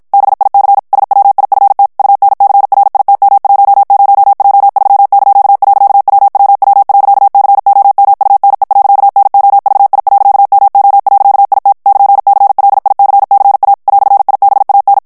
Morse code, heard as a keyed tone, about 54 wpm, 800 Hz. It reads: BIC SWIFT UNCRITM10J3).WWW.COMUNE.NOVI.MO.IT <KN>CLICCA "IBAN
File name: morse mp3 / ITALIEN texte clair 1 vitesse 50 mots minute